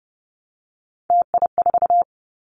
Morse code, heard as a keyed tone, about 30 wpm, 700 Hz. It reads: TI4
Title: Callsign Prefixes for Top 100 DXCC - 30wpm